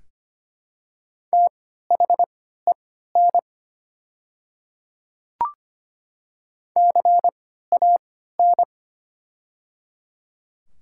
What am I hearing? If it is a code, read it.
Morse code, heard as a keyed tone, about 25 wpm, spaced as for 14 wpm, 700 Hz. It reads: THEN CAN
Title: 3x Character Spacing - Top 100 Words - 25wpm